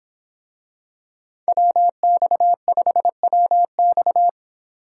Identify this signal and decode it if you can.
Morse code, heard as a keyed tone, about 25 wpm, 700 Hz. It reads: WX5WX